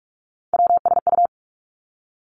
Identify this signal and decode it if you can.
Morse code, heard as a keyed tone, about 46 wpm, 700 Hz. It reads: WSU
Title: Three Challenging Letter-Number Combinations 45wpm